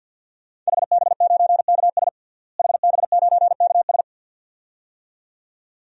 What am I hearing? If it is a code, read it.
Morse code, heard as a keyed tone, about 52 wpm, 700 Hz. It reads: HB9KS HB9KS
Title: DXCC-100 Call Signs - 5 Characters - Repeated 2x - 50wpm